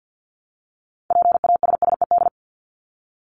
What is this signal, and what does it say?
Morse code, heard as a keyed tone, about 50 wpm, 700 Hz. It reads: PASSED
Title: ED Suffix - Top 100 Words with Base Words 50wpm